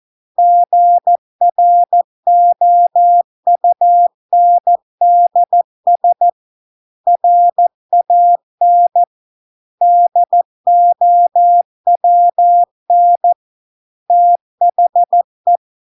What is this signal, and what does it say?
Morse code, heard as a keyed tone, about 14 wpm, 700 Hz. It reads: GROUNDS RAN DOWN THE